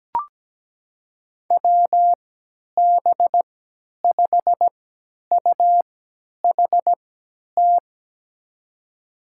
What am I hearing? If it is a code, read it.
Morse code, heard as a keyed tone, about 17 wpm, spaced as for 10 wpm, 700 Hz. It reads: WB5UHT
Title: USA Call Signs - 6 Characters - 3x Character Spacing - 17wpm